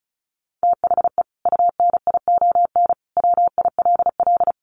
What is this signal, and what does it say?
Morse code, heard as a keyed tone, about 34 wpm, 700 Hz. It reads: THE UNION WILL
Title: Sentences from Top 1500 Words 35wpm